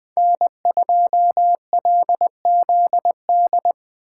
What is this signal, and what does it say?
Morse code, heard as a keyed tone, about 24 wpm, 700 Hz. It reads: N2LZD
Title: Speed Racing - USA 1x3 Call Signs 20wpm